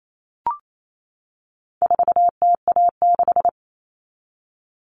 Morse code, heard as a keyed tone, about 28 wpm, 700 Hz. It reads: 4TA6
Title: All Cut Numbers - 4-Digit 28wpm